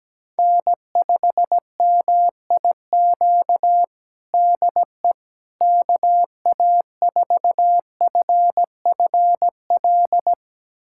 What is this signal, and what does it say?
Morse code, heard as a keyed tone, about 17 wpm, 700 Hz. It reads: N5MIQ DE KA4FFL
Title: USA Callsign DE Callsign 17wpm